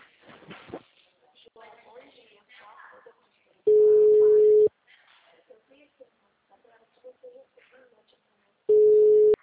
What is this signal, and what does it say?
Ringback tone (Europe, 425 Hz), with speech.